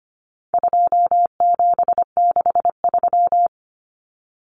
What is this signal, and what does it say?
Morse code, heard as a keyed tone, about 25 wpm, 700 Hz. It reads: 2763
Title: Common Cut Numbers - 4-Digit 25wpm